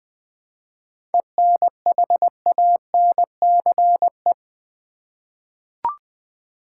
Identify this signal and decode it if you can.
Morse code, heard as a keyed tone, about 20 wpm, 700 Hz. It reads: ENHANCE
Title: 7 Letter Words - First 3 Letters Given - 20wpm